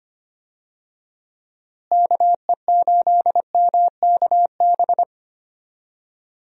Morse code, heard as a keyed tone, about 25 wpm, 700 Hz. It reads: KE8MKB
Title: USA Call Signs - 6 Characters - Repeated 2x - 25wpm